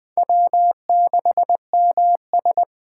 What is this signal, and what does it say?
Morse code, heard as a keyed tone, about 20 wpm, 700 Hz. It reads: W6MS
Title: Speed Racing - USA Call Signs 17wpm